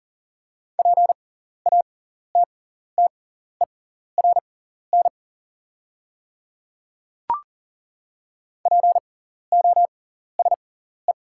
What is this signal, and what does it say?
Morse code, heard as a keyed tone, about 40 wpm, spaced as for 14 wpm, 690 Hz. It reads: PATTERN POSE
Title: 6x Character Spacing - Top 500 Words - 40wpm